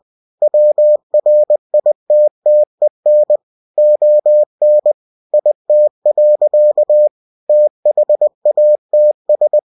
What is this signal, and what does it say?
Morse code, heard as a keyed tone, about 20 wpm, 600 Hz. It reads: WRITTEN ON IT. THATS
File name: cw-books / winnie-the-pooh / winnie-20wpm-page-0057